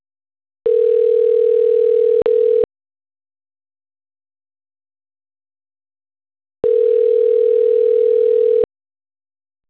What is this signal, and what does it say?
Ringback tone (North America), 440 + 480 Hz.
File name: autotestagent / 10-24-19.136..callcentertransfer. .a -e25d-43d5-8f48